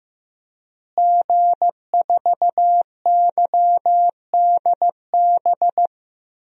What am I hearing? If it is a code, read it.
Morse code, heard as a keyed tone, about 15 wpm, 700 Hz. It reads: G4YDB
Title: DXCC-100 Call Signs - 5 Characters - Repeated 2x - 15wpm